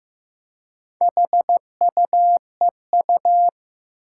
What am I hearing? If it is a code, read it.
Morse code, heard as a keyed tone, about 15 wpm, 700 Hz. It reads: HUEU
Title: Four Challenging Letter-Number Combinations 15wpm